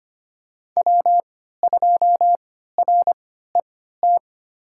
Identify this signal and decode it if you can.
Morse code, heard as a keyed tone, about 25 wpm, spaced as for 14 wpm, 700 Hz. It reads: W2RET
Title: USA Call Signs - 5 Characters - 3x Character Spacing - 25wpm